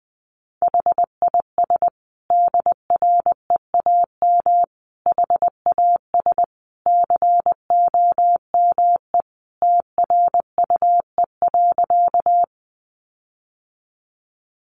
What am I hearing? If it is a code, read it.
Morse code, heard as a keyed tone, about 20 wpm, 700 Hz. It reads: HIS DREAM HAS COME TRUE.
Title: Sentences from Top 1500 Words 20wpm